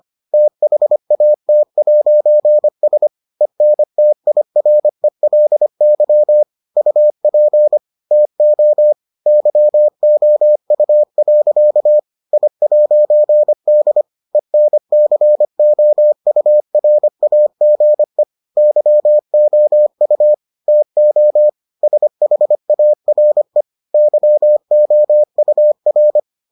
Morse code, heard as a keyed tone, about 25 wpm, 600 Hz. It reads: THAT'S ENTIRELY UP TO YOU. I'D ENCOURAGE YOU TO SHARE YOUR